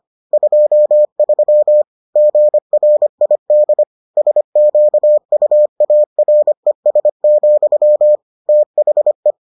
Morse code, heard as a keyed tone, about 25 wpm, 600 Hz. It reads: 23 GRID SQUARES, THE